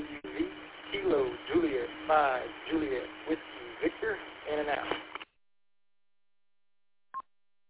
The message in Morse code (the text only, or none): none